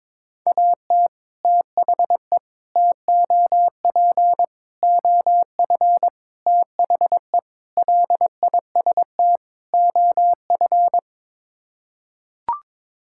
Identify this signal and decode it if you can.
Morse code, heard as a keyed tone, about 22 wpm, 700 Hz. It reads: AT THE TOP OF THE LIST OF
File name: practice / 22wpm / Sets of 7 Words - 22wpm